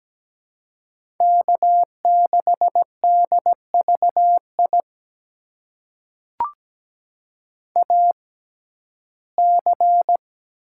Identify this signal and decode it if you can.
Morse code, heard as a keyed tone, about 17 wpm, spaced as for 12 wpm, 700 Hz. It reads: K6DVI AC
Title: USA Call Signs - 5 Characters - 6x Character Spacing - 17wpm